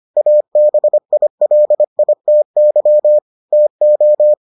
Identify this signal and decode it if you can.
Morse code, heard as a keyed tone, about 25 wpm, 600 Hz. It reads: ABILITY TO